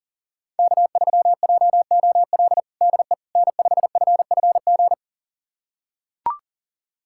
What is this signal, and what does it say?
Morse code, heard as a keyed tone, about 40 wpm, 700 Hz. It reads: K3JOL DE N5FFG
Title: USA Callsign DE Callsign - 5 Characters - 40wpm